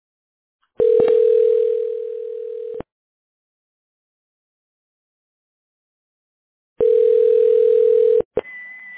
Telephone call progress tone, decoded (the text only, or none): ringback tone